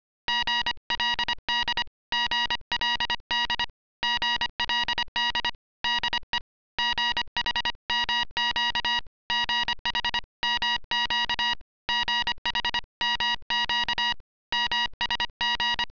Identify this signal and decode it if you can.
Morse code, heard as a keyed tone, about 25 wpm, 2850 Hz, with noise. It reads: GLD GLD GLD DE GHMQ GHMQ GHMQ MSG